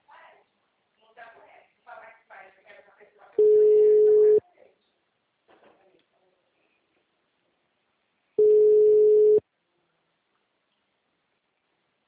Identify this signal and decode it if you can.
Ringback tone (Europe, 425 Hz), with speech.